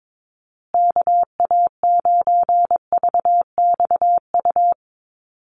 Morse code, heard as a keyed tone, about 22 wpm, 700 Hz. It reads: KA9VXU
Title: USA Call Signs Repeated 2x 22wpm